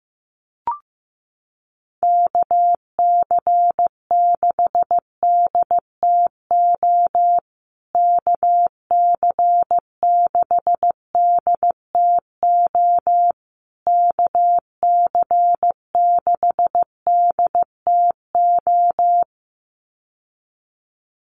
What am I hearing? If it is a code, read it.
Morse code, heard as a keyed tone, about 15 wpm, 700 Hz. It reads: KC6DTO KC6DTO KC6DTO
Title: USA Call Signs Repeated 3x 15wpm